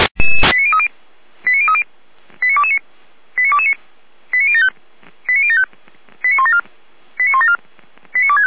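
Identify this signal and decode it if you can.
Five-tone selective call (ZVEI), heard as quick runs of five tones, tones of about 70 ms. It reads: 89030 89030 89229 89229 89075 89075 89174 89174 89265
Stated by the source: ZVEI five-tone call 89165 received by SDR.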